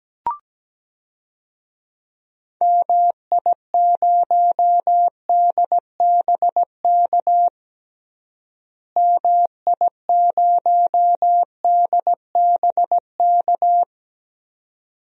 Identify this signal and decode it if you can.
Morse code, heard as a keyed tone, about 17 wpm, 700 Hz. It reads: MI0DBK MI0DBK
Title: DXCC-100 Call Signs - 6 Characters - Repeated 2x - 17wpm